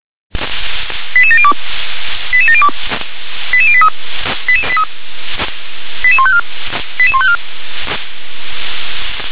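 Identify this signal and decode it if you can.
Five-tone selective call (ZVEI), heard as quick runs of five tones, tones of about 70 ms. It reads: 88702 88702 88083 88154 88154
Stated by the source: ZVEI five-tone selective call received on SDR.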